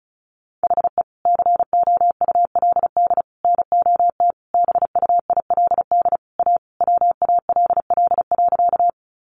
Morse code, heard as a keyed tone, about 34 wpm, 700 Hz. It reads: HE COULD NOT BUILD A WALL.
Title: Sentences from Top 1100 Words 35wpm